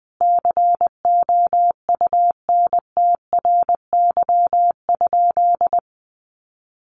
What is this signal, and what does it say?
Morse code, heard as a keyed tone, about 20 wpm, 700 Hz. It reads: COUNTRY?
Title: Sentences from Top 1400 Words 20wpm